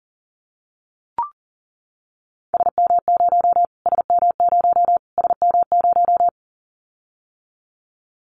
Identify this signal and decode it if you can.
Morse code, heard as a keyed tone, about 40 wpm, 700 Hz. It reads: SM0 SM0 SM0
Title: Callsign Prefixes for Top 25 DXCC - Repeated 3x - 40wpm